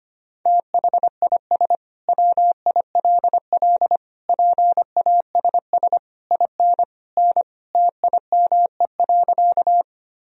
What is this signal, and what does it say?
Morse code, heard as a keyed tone, about 25 wpm, 700 Hz. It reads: THIS WILL PASS IN N TIME.
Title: Sentences from Top 400 Words and Abbreviations 25wpm